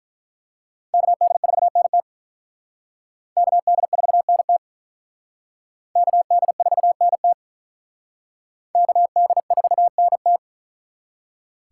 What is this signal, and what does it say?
Morse code, heard as a keyed tone, about 45 wpm, 700 Hz. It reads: KD4NT KD4NT KD4NT KD4NT